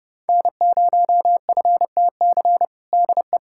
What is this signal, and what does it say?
Morse code, heard as a keyed tone, about 30 wpm, 700 Hz. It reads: N0FTC DE
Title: USA Callsign DE Callsign - 5 Characters - 30wpm